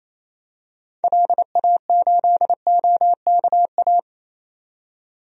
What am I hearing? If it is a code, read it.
Morse code, heard as a keyed tone, about 27 wpm, 700 Hz. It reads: LA8OKA
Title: DXCC-100 Call Signs - 6 Characters - 28wpm